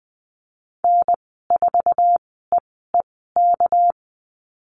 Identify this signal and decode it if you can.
Morse code, heard as a keyed tone, about 20 wpm, spaced as for 14 wpm, 700 Hz. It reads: N4EEK
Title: 245 Lesson 39g - Letter z - Call Signs 20wpm